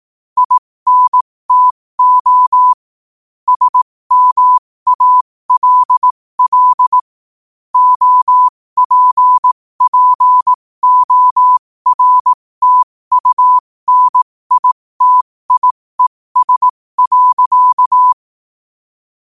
Morse code, heard as a keyed tone, about 18 wpm, spaced as for 15 wpm, 1000 Hz. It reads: INTO SMALL OPPORTUNITIES.
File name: qotdfiles / 15wpm